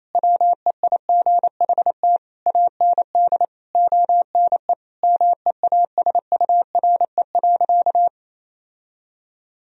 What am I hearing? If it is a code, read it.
Morse code, heard as a keyed tone, about 28 wpm, 700 Hz. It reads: WEIGHT AND ONE MEASURE.
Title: Sentences from Top 1000 Words - Review Part 3 - 28wpm